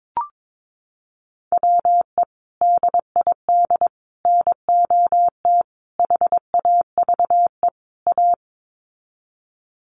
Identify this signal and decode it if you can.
Morse code, heard as a keyed tone, about 22 wpm, 700 Hz. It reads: WE DID NOT HAVE A
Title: Sets of 5 Words Encore2 22wpm